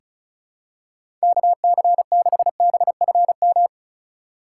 Morse code, heard as a keyed tone, about 36 wpm, 700 Hz. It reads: KC6BFM
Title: USA Call Signs - 6 Characters - 6x Character Spacing - 35wpm